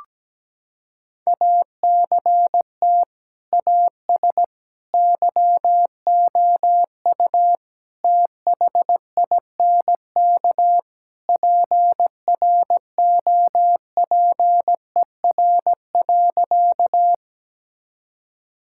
Morse code, heard as a keyed tone, about 17 wpm, 700 Hz. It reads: ACT AS YOU THINK PROPER.
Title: Sentences from Top 1000 Words - Review Part 5 - 17wpm